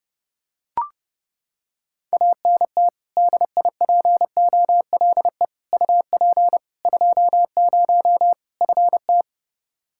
Morse code, heard as a keyed tone, about 30 wpm, 700 Hz. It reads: ANT DIPOLE UP 20 FT
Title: Practice QSO - Exchange 2 - Info - 30wpm